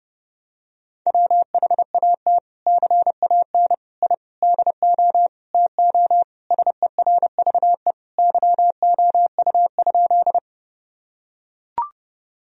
Morse code, heard as a keyed tone, about 32 wpm, 700 Hz. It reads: WHAT CAN I DO TO SERVE YOU?
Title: Sentences from Top 1000 Words - Review Part 2 - 30wpm